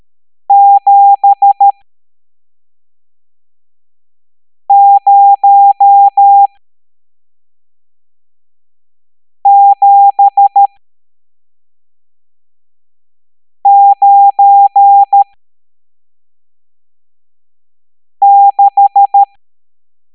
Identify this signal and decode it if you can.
Morse code, heard as a keyed tone, about 13 wpm, spaced as for 3 wpm, 800 Hz. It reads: 70796
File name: hamdata / L11-EX01-67890